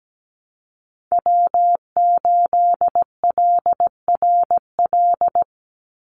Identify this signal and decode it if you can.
Morse code, heard as a keyed tone, about 17 wpm, 700 Hz. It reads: W8LRL
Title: USA Call Signs 17wpm